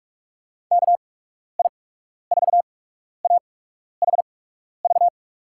Morse code, heard as a keyed tone, about 46 wpm, spaced as for 12 wpm, 700 Hz. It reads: KI4AHV
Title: USA Call Signs - 6 Characters - 8x Character Spacing - 45wpm